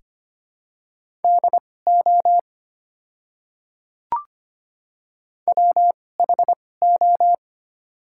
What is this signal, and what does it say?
Morse code, heard as a keyed tone, about 25 wpm, spaced as for 18 wpm, 700 Hz. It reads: DO WHO
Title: 2x Character Spacing - Top 100 Words - 25wpm